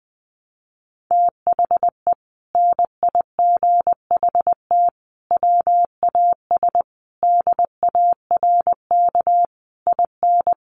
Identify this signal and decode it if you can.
Morse code, heard as a keyed tone, about 20 wpm, 700 Hz. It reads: THE NIGHT WAS DARK IN
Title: Sentences from Top 1300 Words 20wpm